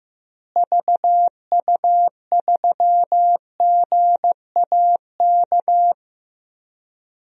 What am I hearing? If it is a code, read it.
Morse code, heard as a keyed tone, about 15 wpm, 700 Hz. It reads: VU3GAK
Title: DXCC-100 Call Signs - 6 Characters - Repeated 2x - 15wpm